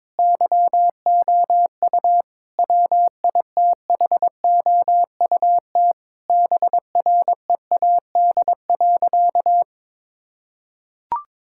Morse code, heard as a keyed tone, about 22 wpm, 700 Hz. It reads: YOU WITHOUT BREAD.
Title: Sentences from Top 1600 Words 22wpm